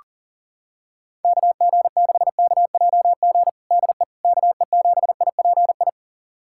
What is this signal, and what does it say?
Morse code, heard as a keyed tone, about 37 wpm, 690 Hz. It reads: KG6KJG DE KE7IPI
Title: USA Callsign DE Callsign - 6 Characters - 40wpm